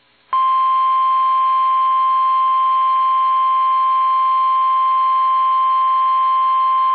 Emergency Alert System (SAME) tones (the 1050 Hz attention tone). It is heard from 0.5 to 6.4 s.
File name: eas / KUSO / KUSO